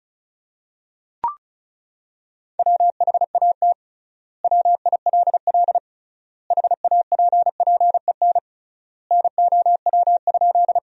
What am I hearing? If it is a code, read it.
Morse code, heard as a keyed tone, about 34 wpm, 700 Hz. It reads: WHAT WILL HAPPEN NOW?
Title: 3x Word-Spacing - Sentences from Top 400 Words - 35wpm